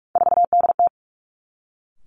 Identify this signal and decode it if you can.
Morse code, heard as a keyed tone, about 46 wpm, 700 Hz. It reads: 4DT
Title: Three Challenging Letter-Number Combinations 45wpm